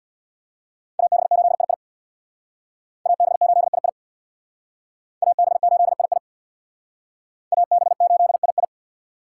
Morse code, heard as a keyed tone, about 57 wpm, 700 Hz. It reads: AB8II AB8II AB8II AB8II
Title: Speed Racing - USA 2x2 Call Signs 50wpm